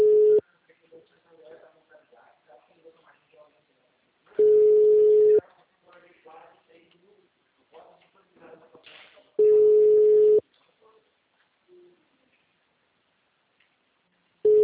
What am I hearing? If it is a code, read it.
Ringback tone (Europe, 425 Hz), with speech.